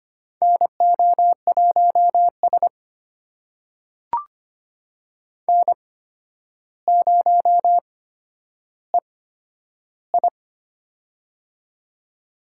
Morse code, heard as a keyed tone, about 25 wpm, spaced as for 11 wpm, 700 Hz. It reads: NO1S N0EI
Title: USA Call Signs - 4 Characters - 8x Character Spacing - 25wpm